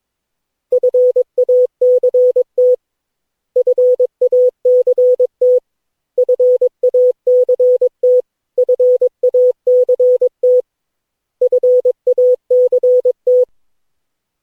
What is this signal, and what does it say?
Morse code, heard as a keyed tone, about 22 wpm, 500 Hz. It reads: FACT FACT FACT FACT FACT
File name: cw / Session9 cw training